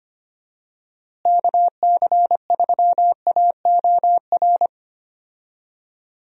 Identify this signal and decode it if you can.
Morse code, heard as a keyed tone, about 25 wpm, 700 Hz. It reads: KC3AOR
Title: USA Call Signs - 6 Characters - Repeated 2x - 25wpm